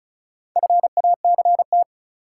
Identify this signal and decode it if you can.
Morse code, heard as a keyed tone, about 34 wpm, 700 Hz. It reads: FACT
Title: Sentences from Top 500 Words - New Word Drill - 35wpm